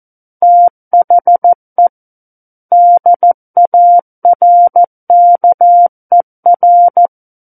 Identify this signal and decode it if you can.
Morse code, heard as a keyed tone, about 14 wpm, 700 Hz. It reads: THE DARKER
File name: Book One / 05-Chapter 5 - 14 WPM